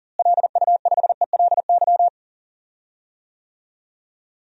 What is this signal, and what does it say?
Morse code, heard as a keyed tone, about 40 wpm, 700 Hz. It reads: LU5ELY
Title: DXCC-100 Call Signs - 6 Characters - Repeated 2x - 40wpm